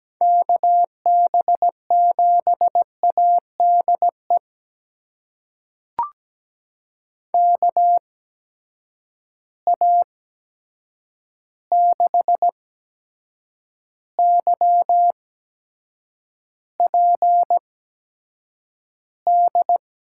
Morse code, heard as a keyed tone, about 17 wpm, spaced as for 7 wpm, 700 Hz. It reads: KB7ADE KA6YPD
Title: USA Call Signs - 6 Characters - 8x Character Spacing - 17wpm